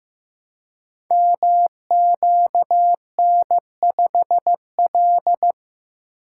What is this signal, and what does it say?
Morse code, heard as a keyed tone, about 15 wpm, 700 Hz. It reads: MQN5L
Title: Code Groups - 5 Letter:Numbers - 15wpm